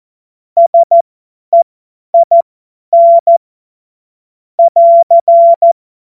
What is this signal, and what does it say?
Morse code, heard as a keyed tone, about 14 wpm, spaced as for 10 wpm, 680 Hz. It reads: SEIN <AR>
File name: Morsekurs / Witze / Witz 11. Mai 2021, 50 F70